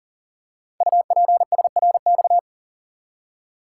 Morse code, heard as a keyed tone, about 42 wpm, 700 Hz. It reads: UPSRX
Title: Code Groups - 5 Letters - 40wpm